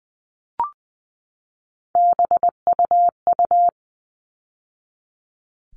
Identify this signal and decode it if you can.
Morse code, heard as a keyed tone, about 20 wpm, 700 Hz. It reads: BUU
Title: All Cut Numbers - 3-Digit 20wpm